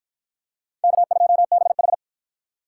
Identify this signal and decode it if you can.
Morse code, heard as a keyed tone, about 48 wpm, 700 Hz. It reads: K2BH